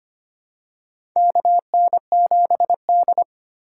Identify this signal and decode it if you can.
Morse code, heard as a keyed tone, about 25 wpm, 700 Hz. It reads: KN7D
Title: K1USN SST QSO Practice - 25wpm